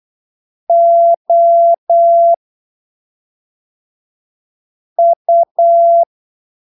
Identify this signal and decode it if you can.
Morse code, heard as a keyed tone, about 8 wpm, spaced as for 3 wpm, 680 Hz. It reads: OU